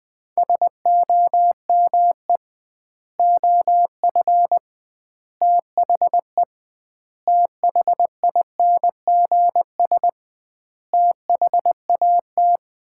Morse code, heard as a keyed tone, about 21 wpm, 700 Hz. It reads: SOME OF THE THINGS THAT
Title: 2x Word-Spacing - Sets of 5 Words - 20wpm